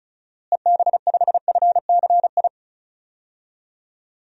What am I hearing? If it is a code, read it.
Morse code, heard as a keyed tone, about 36 wpm, 700 Hz. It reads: EB5FCI
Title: DXCC-100 Call Signs - 6 Characters - Repeated 3x - 35wpm